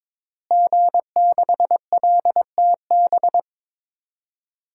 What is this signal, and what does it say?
Morse code, heard as a keyed tone, about 22 wpm, 700 Hz. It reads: G6LTB
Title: DXCC-100 Call Signs - 5 Characters - Repeated 2x - 22wpm